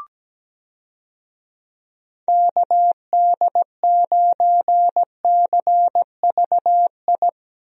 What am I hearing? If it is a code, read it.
Morse code, heard as a keyed tone, about 17 wpm, 700 Hz. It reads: KD9CVI
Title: USA Call Signs - 6 Characters - Repeated 3x - 17wpm